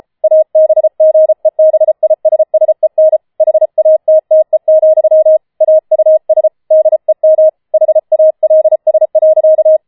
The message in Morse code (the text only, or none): ABGEBISSEN HATTE, AUS DEM HALS.